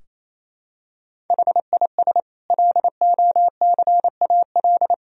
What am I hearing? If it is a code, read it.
Morse code, heard as a keyed tone, about 28 wpm, 700 Hz. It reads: HIS LOCAL
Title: Sentences from Top 1000 Words - Review Part 3 - 28wpm